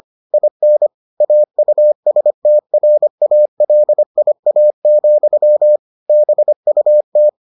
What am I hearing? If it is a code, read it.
Morse code, heard as a keyed tone, about 25 wpm, 600 Hz. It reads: IN AUSTRALIA, BUT